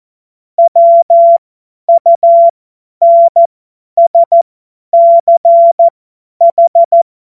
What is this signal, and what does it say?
Morse code, heard as a keyed tone, about 14 wpm, spaced as for 10 wpm, 680 Hz. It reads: WUNSCH
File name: Morsekurs / Witze / Witz 20. Dezember 2022 50 F70